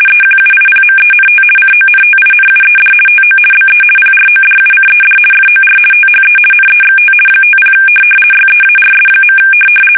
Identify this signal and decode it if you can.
RTTY, about 50 baud. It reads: EQUIPMENT CONTINUED TO BE PRODUCED FOR THE NEXT 50 YEARS.
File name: fsk850-50B / teletype-corp